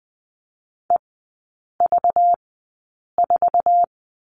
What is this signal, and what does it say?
Morse code, heard as a keyed tone, about 20 wpm, spaced as for 8 wpm, 700 Hz. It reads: EV4
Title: 191 Lesson 31e - Letter v - Three Characters 20wpm